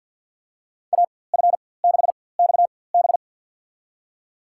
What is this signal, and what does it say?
Morse code, heard as a keyed tone, about 50 wpm, spaced as for 23 wpm, 700 Hz. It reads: AF6XB
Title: USA Call Signs - 5 Characters - 4x Character Spacing - 50wpm